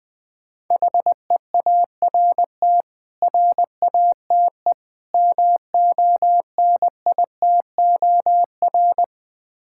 Morse code, heard as a keyed tone, about 21 wpm, 700 Hz. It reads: HEART RATE MONITOR
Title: Difficult Words - Speed-Racing with Hints - 20wpm